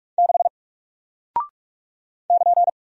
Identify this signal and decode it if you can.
Morse code, heard as a keyed tone, about 44 wpm, 700 Hz. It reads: 6 <KN>
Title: Single Character (Letter:Number:Punctuation:Prosign) Alt 45wpm